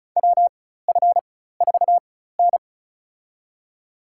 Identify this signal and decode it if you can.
Morse code, heard as a keyed tone, about 36 wpm, spaced as for 16 wpm, 700 Hz. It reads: WF4N
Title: USA Call Signs - 4 Characters - 4x Character Spacing - 35wpm